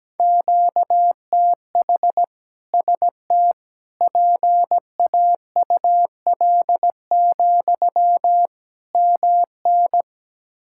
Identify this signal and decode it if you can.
Morse code, heard as a keyed tone, about 17 wpm, 700 Hz. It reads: QTH ST PAUL, MN
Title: K1USN SST QSO Practice 17wpm